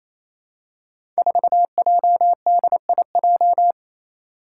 Morse code, heard as a keyed tone, about 28 wpm, 700 Hz. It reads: 4JDIJ
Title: Five Challenging Letter-Number Combinations 28wpm